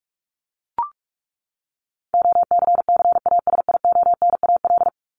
Morse code, heard as a keyed tone, about 44 wpm, 700 Hz. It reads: OCCASIONAL
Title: LY Suffix - Top 100 Words with Base Words 45wpm